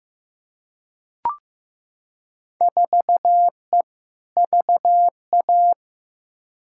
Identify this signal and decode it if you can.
Morse code, heard as a keyed tone, about 15 wpm, 700 Hz. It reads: 4E VA
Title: Field Day Exchanges Encore - 15wpm